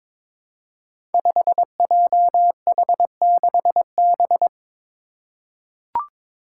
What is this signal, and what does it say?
Morse code, heard as a keyed tone, about 22 wpm, 700 Hz. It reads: 5JH6B
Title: Five Challenging Letter-Number Combinations 22wpm